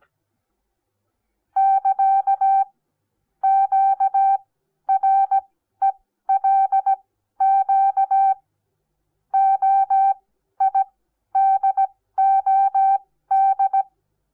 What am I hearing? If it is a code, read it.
Morse code, heard as a keyed tone, about 17 wpm, spaced as for 12 wpm, 780 Hz. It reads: <KA> QRELQ OIDOD
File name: Onlinelehrgang / morsen / DJ4UF Morsen08